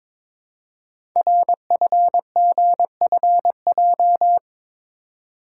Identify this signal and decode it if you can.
Morse code, heard as a keyed tone, about 22 wpm, 700 Hz. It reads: RFGFJ
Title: Code Groups - 5 Letters - 22wpm